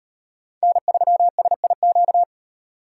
Morse code, heard as a keyed tone, about 38 wpm, 700 Hz. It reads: N3SIQ